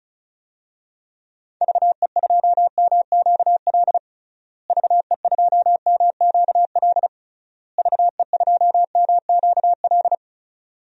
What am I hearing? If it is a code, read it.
Morse code, heard as a keyed tone, about 36 wpm, 700 Hz. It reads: VE2MQL VE2MQL VE2MQL
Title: DXCC-100 Call Signs - 6 Characters - Repeated 3x - 35wpm